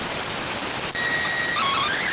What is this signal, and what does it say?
SSTV (slow-scan television), PD 290.